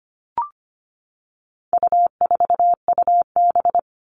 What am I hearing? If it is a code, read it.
Morse code, heard as a keyed tone, about 25 wpm, 700 Hz. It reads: U4UB
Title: All Cut Numbers - 4-Digit 25wpm